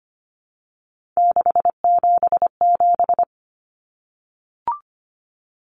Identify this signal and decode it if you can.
Morse code, heard as a keyed tone, about 25 wpm, 700 Hz. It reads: 677
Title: Numbers - 3-Digit 25wpm